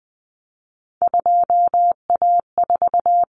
Morse code, heard as a keyed tone, about 20 wpm, 700 Hz. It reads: 2A4
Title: Common Cut Numbers - 3-Digit 20wpm